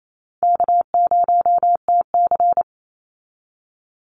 Morse code, heard as a keyed tone, about 28 wpm, 700 Hz. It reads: K0TC
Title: Easy Call Signs Repeated 2x - 28wpm